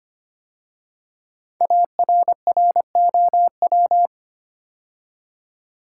Morse code, heard as a keyed tone, about 25 wpm, 700 Hz. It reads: ARROW
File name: 5 Letter Words - First 3 Letters Given - 25wpm